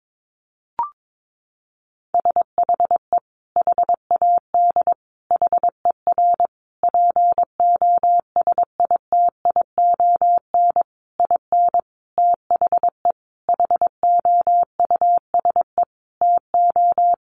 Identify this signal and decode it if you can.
Morse code, heard as a keyed tone, about 22 wpm, 700 Hz. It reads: SHE HAD HER POSITION IN THE HOUSE TO
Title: Sentences from Top 900 Words 22wpm